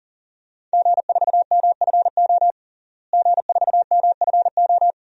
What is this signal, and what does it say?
Morse code, heard as a keyed tone, about 40 wpm, 690 Hz. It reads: G4MFO G4MFO
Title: DXCC-100 Call Signs - 5 Characters - Repeated 2x - 40wpm